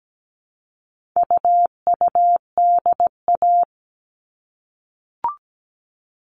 Morse code, heard as a keyed tone, about 17 wpm, 700 Hz. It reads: UUDA
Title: All Cut Numbers - 4-Digit 17wpm